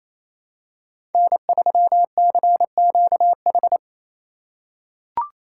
Morse code, heard as a keyed tone, about 27 wpm, 700 Hz. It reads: N3CQH
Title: USA Call Signs - 5 Characters - 2x Character Spacing - 28wpm